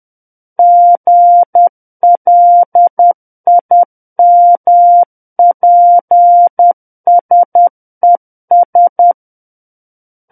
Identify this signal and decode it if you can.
Morse code, heard as a keyed tone, about 10 wpm, 700 Hz. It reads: GLIMPSES
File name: Book One / 01-Chapter 1 - 10 WPM